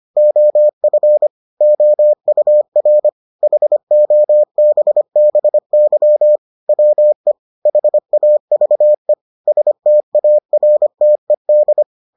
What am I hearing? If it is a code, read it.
Morse code, heard as a keyed tone, about 25 wpm, 600 Hz. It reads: OF OUR HOBBY WE HAVE STARTED